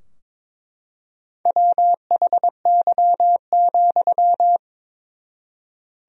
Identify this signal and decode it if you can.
Morse code, heard as a keyed tone, about 22 wpm, 700 Hz. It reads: WHY,
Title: 4x Word-Spacing - Sentences from Top 200 Words - 22wpm